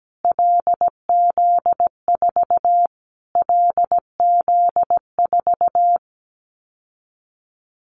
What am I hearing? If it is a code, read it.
Morse code, heard as a keyed tone, about 17 wpm, 700 Hz. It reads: LZ4 LZ4
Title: Callsign Prefixes for Top 25 DXCC - Repeated 3x - 17wpm